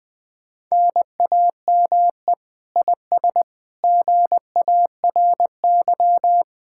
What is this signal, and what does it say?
Morse code, heard as a keyed tone, about 20 wpm, 700 Hz. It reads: NAME IS GARY
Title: Speed-Racing Common QSO Phrases - 17wpm